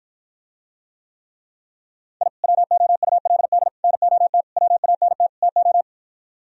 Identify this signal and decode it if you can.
Morse code, heard as a keyed tone, about 52 wpm, 700 Hz. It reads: I WOULD NOT WANT TO